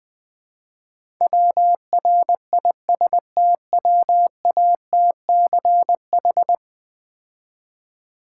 Morse code, heard as a keyed tone, about 20 wpm, 700 Hz. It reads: WRISTWATCH
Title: Easy Words - Speed-Racing with Hints - 20wpm